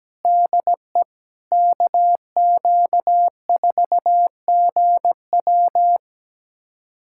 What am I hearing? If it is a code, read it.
Morse code, heard as a keyed tone, about 17 wpm, 700 Hz. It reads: DE KQ4GW
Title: USA Callsign DE Callsign - 5 Characters - 17wpm